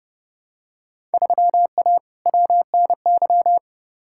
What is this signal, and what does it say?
Morse code, heard as a keyed tone, about 30 wpm, 700 Hz. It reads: 3A WNY
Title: Field Day Exchanges - 30wpm